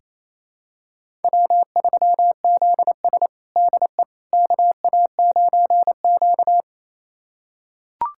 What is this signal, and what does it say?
Morse code, heard as a keyed tone, about 27 wpm, 700 Hz. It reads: W3ZS DE KA9Q
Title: USA Callsign DE Callsign - 4 Characters - 28wpm